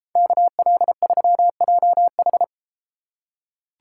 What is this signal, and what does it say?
Morse code, heard as a keyed tone, about 32 wpm, 700 Hz. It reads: KL3JH